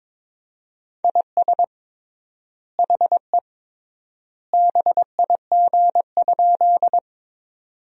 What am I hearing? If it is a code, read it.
Morse code, heard as a keyed tone, about 22 wpm, 700 Hz. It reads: IS HE BIG?
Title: 3x Word-Spacing - Sentences from Top 200 Words - 22wpm